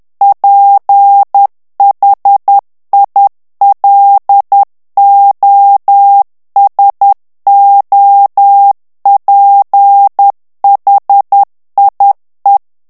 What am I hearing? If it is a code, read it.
Morse code, heard as a keyed tone, about 10 wpm, 800 Hz. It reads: PHILOSOPHIE
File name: morse mp3 / FRANCAIS TEXTE CLAIR 2 VITESSE 10 MOTS MINUTES